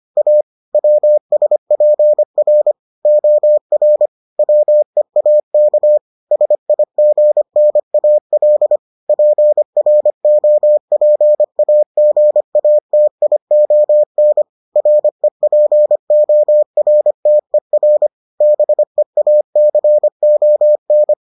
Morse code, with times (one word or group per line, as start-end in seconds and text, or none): 0.2-0.4 A
0.7-2.7 WSPR
3.0-4.1 OR
4.4-6.0 WEAK
6.3-8.8 SIGNAL
9.1-14.4 PROPAGATION
14.8-18.1 REPORTER
18.4-21.1 BEACON